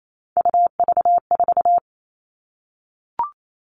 Morse code, heard as a keyed tone, about 28 wpm, 700 Hz. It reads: UV4
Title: All Cut Numbers - 3-Digit 28wpm